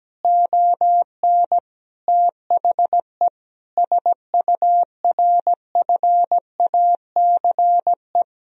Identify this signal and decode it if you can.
Morse code, heard as a keyed tone, about 17 wpm, 700 Hz. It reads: ON THE SURFACE
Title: Sentences from Top 1000 Words - Review Part 3 - 17wpm